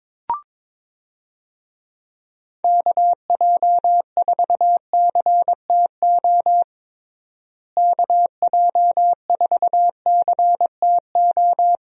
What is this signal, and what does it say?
Morse code, heard as a keyed tone, about 22 wpm, 700 Hz. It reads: KJ4CTO KJ4CTO
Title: USA Call Signs - 6 Characters - Repeated 2x - 22wpm